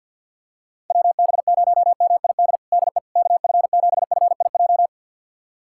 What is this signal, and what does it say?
Morse code, heard as a keyed tone, about 52 wpm, 700 Hz. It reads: WB0MID DE KF7FIJ
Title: USA Callsign DE Callsign - 6 Characters - 50wpm